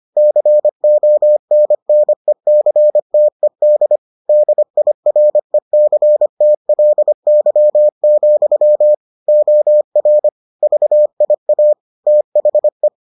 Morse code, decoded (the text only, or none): CONNECTED DIRECTLY, OR VIA THE